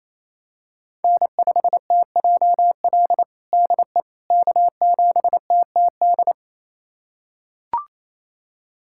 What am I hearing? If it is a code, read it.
Morse code, heard as a keyed tone, about 28 wpm, 700 Hz. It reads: N5TJL DE K7TTD
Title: USA Callsign DE Callsign - 5 Characters - 28wpm